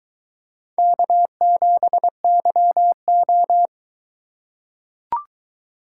Morse code, heard as a keyed tone, about 22 wpm, 700 Hz. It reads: K7YO